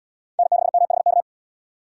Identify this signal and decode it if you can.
Morse code, heard as a keyed tone, about 72 wpm, 700 Hz. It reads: N6MHK